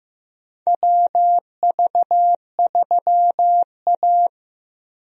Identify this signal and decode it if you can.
Morse code, heard as a keyed tone, about 15 wpm, 700 Hz. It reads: WV3A
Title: USA Call Signs - 4 Characters - Repeated 3x - 15wpm